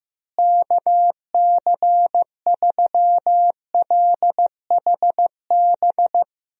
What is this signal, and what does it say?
Morse code, heard as a keyed tone, about 15 wpm, 700 Hz. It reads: KC3LHB
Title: USA Call Signs - 6 Characters - Repeated 3x - 15wpm